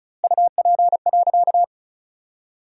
Morse code, heard as a keyed tone, about 34 wpm, 700 Hz. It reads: UP.
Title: 2x Word-Spacing - Sentences from Top 500 Words - 35wpm